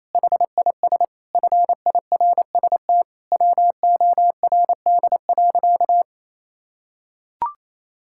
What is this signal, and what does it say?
Morse code, heard as a keyed tone, about 28 wpm, 700 Hz. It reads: HIS FIRST WORD.
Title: Pangram Challenge v2 - 28wpm